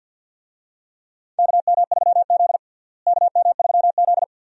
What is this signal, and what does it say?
Morse code, heard as a keyed tone, about 51 wpm, 700 Hz. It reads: KM3Z KM3Z
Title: USA Call Signs - 4 Characters - Repeated 2x - 50wpm